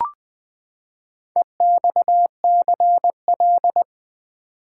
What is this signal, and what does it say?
Morse code, heard as a keyed tone, about 20 wpm, 700 Hz. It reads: EXCL
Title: Build A Word - 4 Letters At A Time - 11-Letter Words - 20wpm